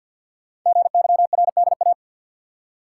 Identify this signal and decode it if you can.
Morse code, heard as a keyed tone, about 52 wpm, 690 Hz. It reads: GYRDA
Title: Code Groups - 5 Letters - 50wpm